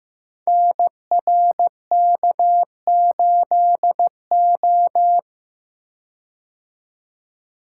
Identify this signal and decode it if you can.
Morse code, heard as a keyed tone, about 15 wpm, 700 Hz. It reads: NRK8O
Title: Code Groups - 5 Letter:Numbers - 15wpm